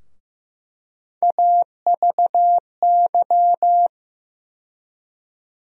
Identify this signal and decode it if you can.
Morse code, heard as a keyed tone, about 15 wpm, 700 Hz. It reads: AVY
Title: USA Call Sign Postfixes - 3 Letters 15wpm